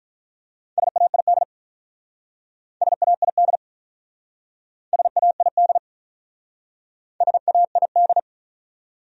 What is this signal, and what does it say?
Morse code, heard as a keyed tone, about 44 wpm, 700 Hz. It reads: SAID SAID SAID SAID